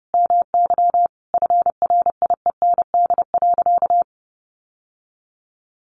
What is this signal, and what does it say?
Morse code, heard as a keyed tone, about 30 wpm, 700 Hz. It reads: MY FRIEND.
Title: Sentences from Top 1100 Words 30wpm